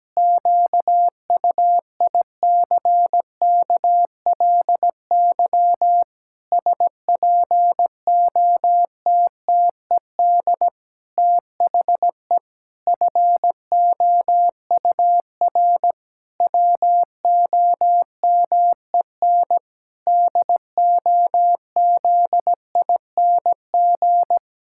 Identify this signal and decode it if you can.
Morse code, heard as a keyed tone, about 17 wpm, 700 Hz. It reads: QUICKLY SPOTTED THE FOUR WOMEN DOZING
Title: Pangram Challenge v2 - 17wpm